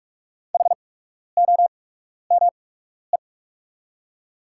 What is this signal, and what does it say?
Morse code, heard as a keyed tone, about 44 wpm, spaced as for 12 wpm, 690 Hz. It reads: HOME
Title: 8x Character Spacing - Top 200 Words - 45wpm